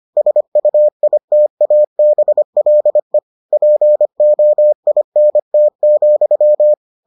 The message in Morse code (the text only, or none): SUITABLE POINT,